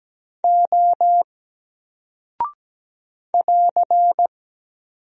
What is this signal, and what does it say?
Morse code, heard as a keyed tone, about 17 wpm, 700 Hz. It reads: O <AR>
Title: Single Character (Letter:Number:Punctuation:Prosign) 17wpm